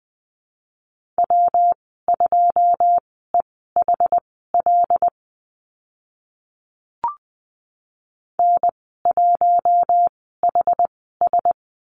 Morse code, heard as a keyed tone, about 20 wpm, spaced as for 14 wpm, 700 Hz. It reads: W2EHL N1HS
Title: 128 Lesson 21g - Number 6 - Call Signs 20wpm